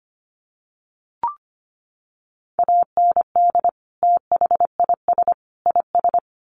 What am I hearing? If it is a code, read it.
Morse code, heard as a keyed tone, about 25 wpm, 700 Hz. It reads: AND THIS IS
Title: Sets of 3 Words 25wpm